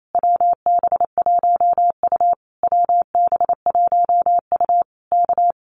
Morse code, heard as a keyed tone, about 28 wpm, 700 Hz. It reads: WB1U WB1U K
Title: Common QSO Phrases Encore 28wpm